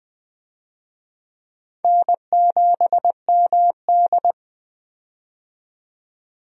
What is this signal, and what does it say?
Morse code, heard as a keyed tone, about 20 wpm, 700 Hz. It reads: N7MD
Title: USA Call Signs - 4 Characters - 20wpm